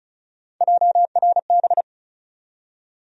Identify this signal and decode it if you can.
Morse code, heard as a keyed tone, about 36 wpm, 700 Hz. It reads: JRB
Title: USA Call Sign Postfixes - 3 Letters 35wpm